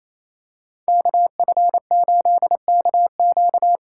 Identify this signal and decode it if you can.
Morse code, heard as a keyed tone, about 28 wpm, 700 Hz. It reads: KF8KQ
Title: USA Call Signs - 4x Character Spacing - 28wpm